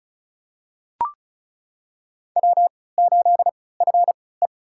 Morse code, heard as a keyed tone, about 36 wpm, spaced as for 20 wpm, 700 Hz. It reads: W8FE
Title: USA Call Signs - 4 Characters - 3x Character Spacing - 35wpm